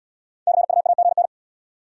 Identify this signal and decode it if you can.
Morse code, heard as a keyed tone, about 80 wpm, 700 Hz. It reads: CUAGN